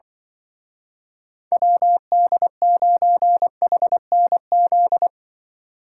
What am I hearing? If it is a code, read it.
Morse code, heard as a keyed tone, about 25 wpm, 700 Hz. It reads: WD9HNZ
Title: Speed Racing - USA 2x3 Call Signs 20wpm